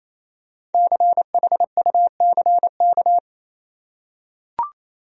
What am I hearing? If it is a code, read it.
Morse code, heard as a keyed tone, about 28 wpm, 700 Hz. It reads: CHUCK
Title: USA Common Names - 28wpm